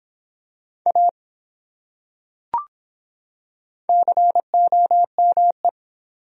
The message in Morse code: A COME

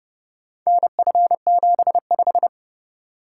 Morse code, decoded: NF75

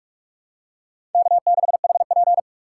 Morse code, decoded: K6HP

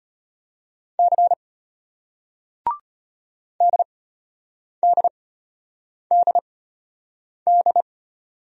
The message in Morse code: C DDDD

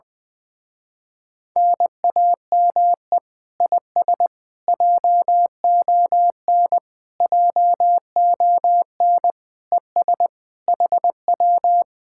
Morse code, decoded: NAME IS JON JON ES HW

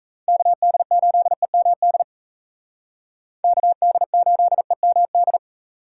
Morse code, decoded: KD8EMD KD8EMD

